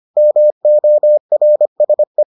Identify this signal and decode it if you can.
Morse code, heard as a keyed tone, about 25 wpm, 600 Hz. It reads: MORSE